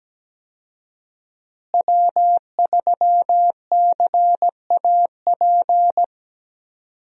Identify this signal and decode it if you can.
Morse code, heard as a keyed tone, about 17 wpm, 700 Hz. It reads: W3CAP
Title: USA Call Signs - 5 Characters - Repeated 3x - 17wpm